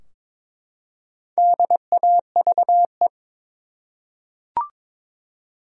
Morse code, heard as a keyed tone, about 22 wpm, 700 Hz. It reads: DAVE
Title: USA Common Names - Max 5 Letters - 22wpm